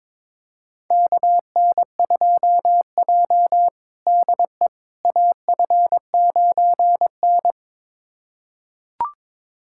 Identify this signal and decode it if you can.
Morse code, heard as a keyed tone, about 22 wpm, 700 Hz. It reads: KN2J DE AF9N
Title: USA Callsign DE Callsign - 4 Characters - 22wpm